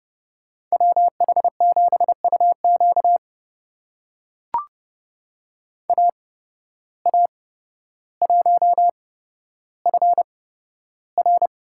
Morse code, heard as a keyed tone, about 32 wpm, spaced as for 13 wpm, 700 Hz. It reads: WH7UQ AA1FR
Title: USA Call Signs - 5 Characters - 8x Character Spacing - 30wpm